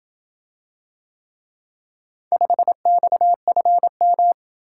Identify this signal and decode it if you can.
Morse code, heard as a keyed tone, about 26 wpm, 700 Hz. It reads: 5XFM